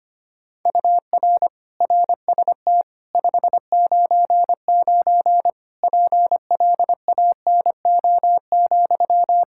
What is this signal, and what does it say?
Morse code, heard as a keyed tone, about 25 wpm, 700 Hz. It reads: UR RST 599 PLANO,
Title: K1USN SST QSO Practice 25wpm